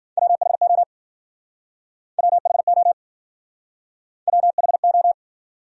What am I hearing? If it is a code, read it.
Morse code, heard as a keyed tone, about 56 wpm, 700 Hz. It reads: WHO WHO WHO